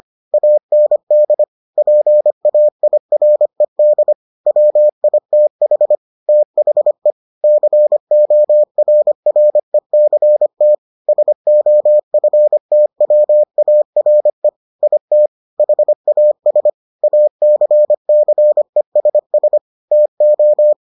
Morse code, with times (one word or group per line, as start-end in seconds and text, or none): 0.3-1.4 AND
1.8-4.1 PAIRED
4.5-6.0 WITH
6.3-7.1 THE
7.4-10.7 CORRECT
11.1-14.5 SOFTWARE
14.8-15.3 IT
15.6-16.7 HAS
17.0-19.6 ACCESS
19.9-20.7 TO